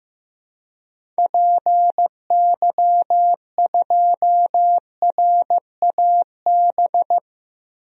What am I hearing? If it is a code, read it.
Morse code, heard as a keyed tone, about 15 wpm, 700 Hz. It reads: PY2RAB
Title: DXCC-100 Call Signs - 6 Characters - 15wpm